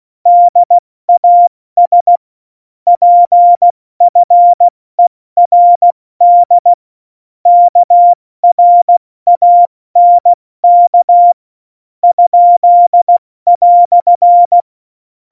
Morse code, heard as a keyed tone, about 16 wpm, spaced as for 13 wpm, 700 Hz. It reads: DAS PFERD KRANK?"
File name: Morsekurs / Witze / Witz 28 Juli 2020, 70 F80